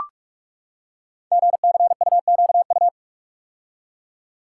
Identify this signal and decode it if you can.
Morse code, heard as a keyed tone, about 44 wpm, 700 Hz. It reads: GCUQU
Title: Code Groups - 5 Letters - 45wpm